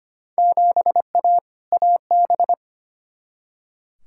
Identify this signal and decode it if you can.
Morse code, heard as a keyed tone, about 25 wpm, 700 Hz. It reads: 7A AB
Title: Field Day Exchanges Encore - 25wpm